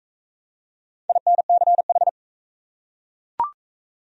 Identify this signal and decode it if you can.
Morse code, heard as a keyed tone, about 39 wpm, 700 Hz. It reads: INCH